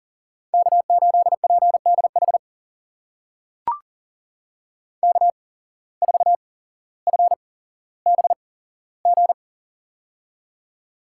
Morse code, heard as a keyed tone, about 40 wpm, spaced as for 17 wpm, 700 Hz. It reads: K8PDH K4FBG
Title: USA Call Signs - 5 Characters - 8x Character Spacing - 40wpm